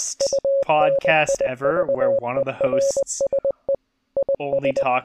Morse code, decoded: UNIVERSE IS